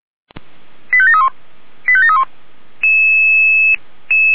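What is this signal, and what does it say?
Five-tone selective call (ZVEI), heard as quick runs of five tones, tones of about 70 ms. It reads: 85731 85731
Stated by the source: ZVEI five-tone selective call received by SDR.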